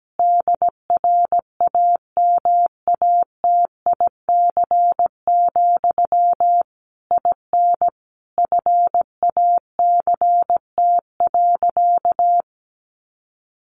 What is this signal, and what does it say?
Morse code, heard as a keyed tone, about 17 wpm, 700 Hz. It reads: DRAMATIC, IN FACT.
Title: Sentences from Top 1300 Words 17wpm